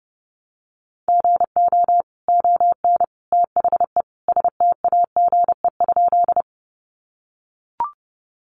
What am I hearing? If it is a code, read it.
Morse code, heard as a keyed tone, about 30 wpm, 700 Hz. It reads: GO ON THE STAGE?
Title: Sentences from Top 1200 Words 30wpm